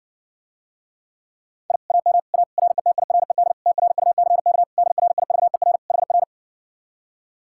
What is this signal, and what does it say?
Morse code, heard as a keyed tone, about 60 wpm, 700 Hz. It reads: I AM A RETIRED TRUCK DRIVER HR